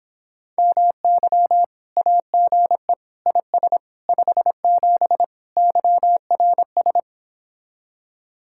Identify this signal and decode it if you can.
Morse code, heard as a keyed tone, about 26 wpm, 700 Hz. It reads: MY AGE IS 57 YRS